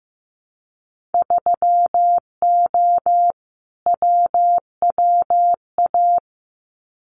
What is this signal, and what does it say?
Morse code, heard as a keyed tone, about 15 wpm, 700 Hz. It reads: 3O WWA
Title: Winter Field Day Exchanges Encore 15wpm